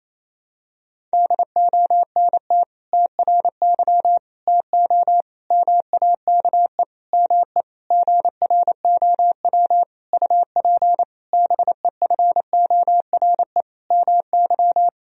Morse code, heard as a keyed tone, about 28 wpm, 700 Hz. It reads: DONT TRY TO MAKE ME GROW UP BEFORE MY
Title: Sentences from Top 1000 Words - Review Part 1 - 28wpm